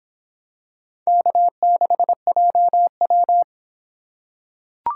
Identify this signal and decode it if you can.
Morse code, heard as a keyed tone, about 24 wpm, 700 Hz. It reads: K6JW